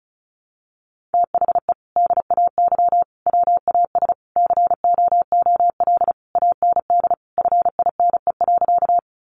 Morse code, heard as a keyed tone, about 34 wpm, 700 Hz. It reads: THE DAY WAS COOL AND FINE.
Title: Sentences from Top 1200 Words 35wpm